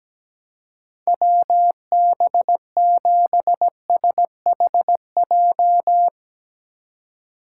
Morse code, heard as a keyed tone, about 17 wpm, 700 Hz. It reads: WB7SHJ
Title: USA Call Signs - 6 Characters - 4x Character Spacing - 17wpm